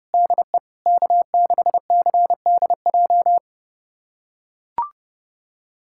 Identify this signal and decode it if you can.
Morse code, heard as a keyed tone, about 30 wpm, 700 Hz. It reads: DE K6CDJ
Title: USA Callsign DE Callsign 30wpm